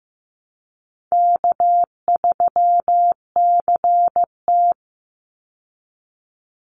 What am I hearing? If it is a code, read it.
Morse code, heard as a keyed tone, about 15 wpm, 700 Hz. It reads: K3CT
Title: Call Signs Repeated 3x - 15wpm